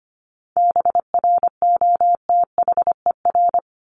Morse code, heard as a keyed tone, about 25 wpm, 700 Hz. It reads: BROTHER
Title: Top 1751-2000 Words 25wpm